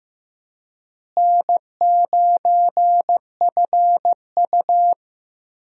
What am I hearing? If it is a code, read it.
Morse code, heard as a keyed tone, about 15 wpm, 700 Hz. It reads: N9FU
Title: USA Call Signs - 4 Characters - Repeated 3x - 15wpm